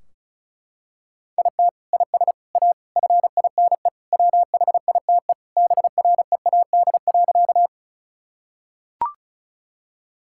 Morse code, heard as a keyed tone, about 36 wpm, 700 Hz. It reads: IT IS A FINE WHITE BREAD.
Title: Sentences from Top 1600 Words 35wpm